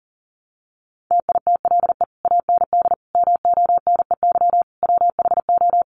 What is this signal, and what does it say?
Morse code, heard as a keyed tone, about 40 wpm, 700 Hz. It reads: TITLE AND MONEY WHO
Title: Sentences from Top 1200 Words 40wpm